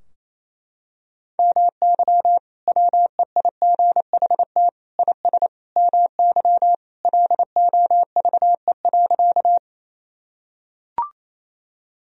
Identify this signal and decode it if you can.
Morse code, heard as a keyed tone, about 28 wpm, 700 Hz. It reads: MY WEIGHT IS MY LOVE.
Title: Sentences from Top 1000 Words - Review Part 3 - 28wpm